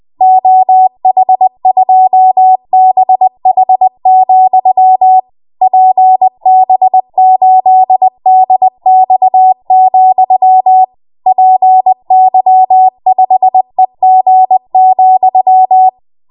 Morse code, heard as a keyed tone, about 20 wpm, 750 Hz. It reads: OH2BH, PB8DX, PY5EG,
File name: Morse / 20WPM